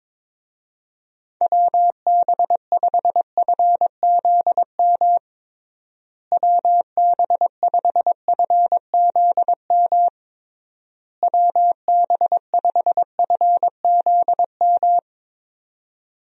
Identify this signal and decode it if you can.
Morse code, heard as a keyed tone, about 22 wpm, 700 Hz. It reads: WB5FZM WB5FZM WB5FZM
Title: USA Call Signs - 6 Characters - Repeated 3x - 22wpm